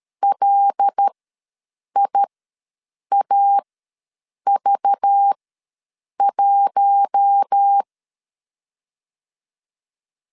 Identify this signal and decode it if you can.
Morse code, heard as a keyed tone, about 13 wpm, spaced as for 7 wpm, 800 Hz. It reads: LIAV1
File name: LEZIONI / 6 PKX0 / 6 PKX ws 12wpm 3106